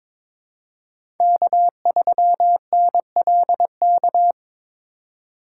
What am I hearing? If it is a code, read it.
Morse code, heard as a keyed tone, about 22 wpm, 700 Hz. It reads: K3NLK
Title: USA Call Signs - 3x Character Spacing - 22wpm